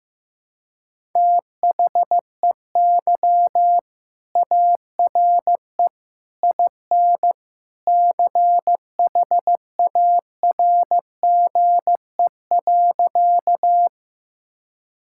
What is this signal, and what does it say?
Morse code, heard as a keyed tone, about 16 wpm, 700 Hz. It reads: THEY ARE IN CHARGE.
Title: Sentences from Top 1000 Words - Review Part 6 - 15wpm